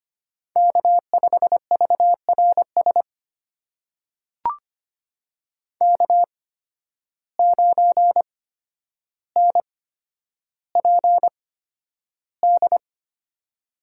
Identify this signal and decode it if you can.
Morse code, heard as a keyed tone, about 25 wpm, spaced as for 11 wpm, 700 Hz. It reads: K5VRS K9NPD
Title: USA Call Signs - 5 Characters - 8x Character Spacing - 25wpm